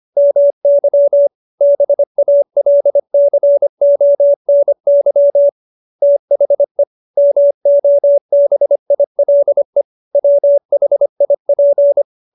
Morse code, heard as a keyed tone, about 25 wpm, 570 Hz. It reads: MY BALCONY THE MOBILE WHIP